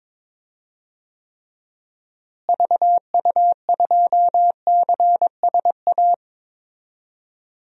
Morse code, heard as a keyed tone, about 22 wpm, 700 Hz. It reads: VU2CSA
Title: DXCC-100 Call Signs - 6 Characters - Repeated 2x - 22wpm